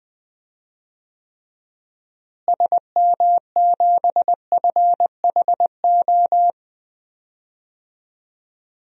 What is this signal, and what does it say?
Morse code, heard as a keyed tone, about 20 wpm, 700 Hz. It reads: SM7FHO
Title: DXCC-100 Call Signs - 6 Characters - Repeated 3x - 20wpm